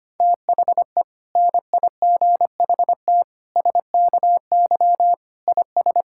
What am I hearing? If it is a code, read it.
Morse code, heard as a keyed tone, about 25 wpm, 700 Hz. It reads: THE NIGHT SKY IS
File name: practice / 25wpm / Difficult Sentences with Hints - 25wpm